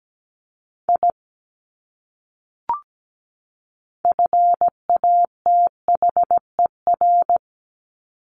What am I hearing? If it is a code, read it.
Morse code, heard as a keyed tone, about 17 wpm, 700 Hz. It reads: I FATHER
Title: Top 200 Words 17wpm